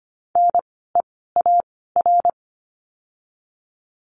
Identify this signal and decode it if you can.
Morse code, heard as a keyed tone, about 25 wpm, 700 Hz. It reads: N E A R
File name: course / 25:10wpm / 094 Lesson 16f - Letter c - Words 25wpm